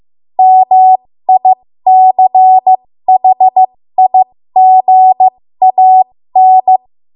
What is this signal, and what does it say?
Morse code, heard as a keyed tone, about 15 wpm, spaced as for 13 wpm, 750 Hz. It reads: MICHIGAN
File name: Morse / 13WPM